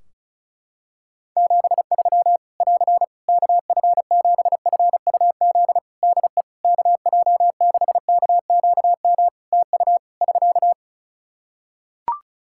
Morse code, heard as a keyed tone, about 34 wpm, 700 Hz. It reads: 73 <AR> KF7FUZ DE KJ6KQM TU <SK>
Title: Practice QSO - Exchange 4 - Ending - 35wpm